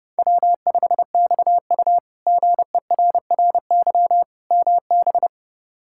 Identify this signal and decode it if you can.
Morse code, heard as a keyed tone, about 32 wpm, 700 Hz. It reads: W5XU GERRY MB
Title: K1USN SST QSO Practice - 30wpm